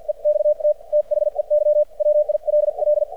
Morse code, heard as a keyed tone, about 50 wpm, spaced as for 42 wpm, 600 Hz. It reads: EXT THEY WILL
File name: archived / CRSnet